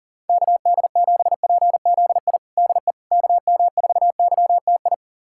Morse code, heard as a keyed tone, about 40 wpm, 700 Hz. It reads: KD7PZI DE KM4YTI
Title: USA Callsign DE Callsign - 6 Characters - 40wpm